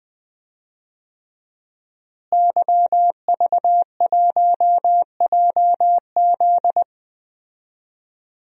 Morse code, heard as a keyed tone, about 20 wpm, 700 Hz. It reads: YV1JZ
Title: DXCC-100 Call Signs - 5 Characters - Repeated 3x - 20wpm